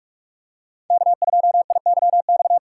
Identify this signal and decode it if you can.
Morse code, heard as a keyed tone, about 44 wpm, 700 Hz. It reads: K2IYX